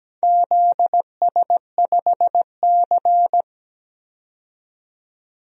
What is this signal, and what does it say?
Morse code, heard as a keyed tone, about 17 wpm, 700 Hz. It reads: ZS5C
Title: DXCC-100 Call Signs - 4 Characters - 17wpm